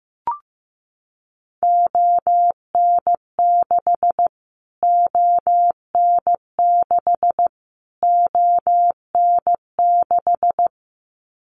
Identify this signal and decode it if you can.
Morse code, heard as a keyed tone, about 15 wpm, 700 Hz. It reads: ON6 ON6 ON6
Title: Callsign Prefixes for Top 25 DXCC - Repeated 3x - 15wpm